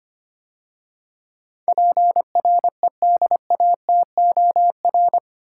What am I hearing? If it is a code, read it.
Morse code, heard as a keyed tone, about 25 wpm, 700 Hz. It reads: PREDATOR
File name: 8 Letter Words - First 3 Letters Given - 25wpm